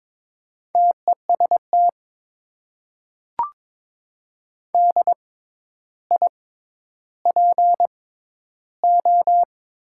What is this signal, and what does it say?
Morse code, heard as a keyed tone, about 22 wpm, 700 Hz. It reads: T E S T D I P O
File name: Top 90 QSO Elements - 6x Character Spacing - 22wpm